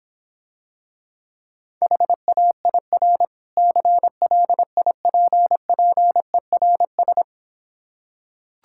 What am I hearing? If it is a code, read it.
Morse code, heard as a keyed tone, about 26 wpm, 700 Hz. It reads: HAIR CLIPPERS